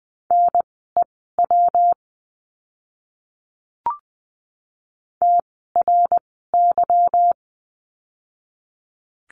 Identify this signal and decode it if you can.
Morse code, heard as a keyed tone, about 20 wpm, spaced as for 14 wpm, 700 Hz. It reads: NEW TRY
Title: 160 Lesson 26f - Letter g - Words 20wpm